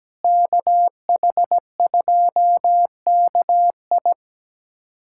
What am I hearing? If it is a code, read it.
Morse code, heard as a keyed tone, about 17 wpm, 700 Hz. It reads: KH2KI
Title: USA Callsign DE Callsign - 5 Characters - 17wpm